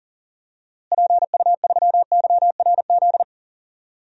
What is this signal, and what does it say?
Morse code, heard as a keyed tone, about 40 wpm, 700 Hz. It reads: PU3YRZ
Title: DXCC-100 Call Signs - 6 Characters - Repeated 3x - 40wpm